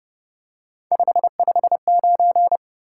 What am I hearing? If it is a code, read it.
Morse code, heard as a keyed tone, about 30 wpm, 700 Hz. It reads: 559
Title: Top 180 QSO Elements 30wpm